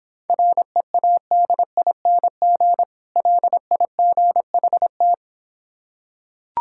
Morse code, heard as a keyed tone, about 26 wpm, 700 Hz. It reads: READING LIGHT